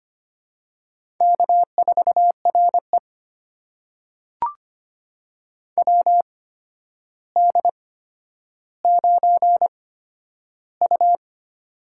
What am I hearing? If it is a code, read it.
Morse code, heard as a keyed tone, about 25 wpm, spaced as for 11 wpm, 700 Hz. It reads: K4RE WD9U
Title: USA Call Signs - 4 Characters - 8x Character Spacing - 25wpm